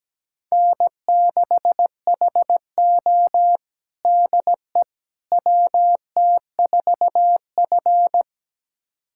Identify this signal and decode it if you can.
Morse code, heard as a keyed tone, about 17 wpm, 700 Hz. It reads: N6HO DE WT4F
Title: USA Callsign DE Callsign - 4 Characters - 17wpm